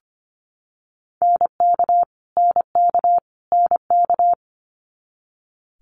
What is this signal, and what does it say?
Morse code, heard as a keyed tone, about 25 wpm, 700 Hz. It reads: NK NK NK
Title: Multi-letter Phonemes Repeated 3x 25wpm